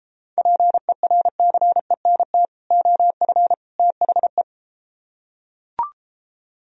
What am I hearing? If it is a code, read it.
Morse code, heard as a keyed tone, about 33 wpm, 700 Hz. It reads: PERCENT OF THE